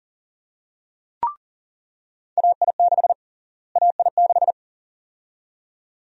Morse code, heard as a keyed tone, about 40 wpm, 700 Hz. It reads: AI6 AI6
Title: USA Call Sign Prefixes - Repeated 2x - 40wpm